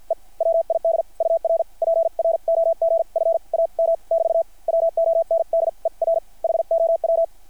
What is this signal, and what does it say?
Morse code, heard as a keyed tone, about 50 wpm, spaced as for 42 wpm, 660 Hz. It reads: E WID UR PROG FAM <BT> WONDER HOW
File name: archived / CRSnet